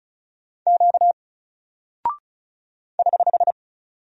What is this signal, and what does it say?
Morse code, heard as a keyed tone, about 36 wpm, 700 Hz. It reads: Q <HH>
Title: Single Character (Letter:Number:Punctuation:Prosign) Alt 35wpm